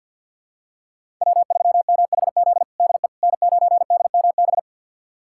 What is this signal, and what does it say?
Morse code, heard as a keyed tone, about 52 wpm, 690 Hz. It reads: W3MHZ DE N9DMB
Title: USA Callsign DE Callsign - 5 Characters - 50wpm